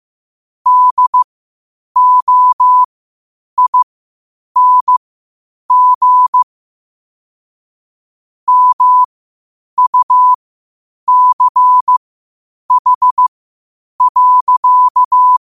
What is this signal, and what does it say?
Morse code, heard as a keyed tone, about 15 wpm, spaced as for 8 wpm, 1000 Hz. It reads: DOING MUCH.
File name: qotdfiles / 8wpm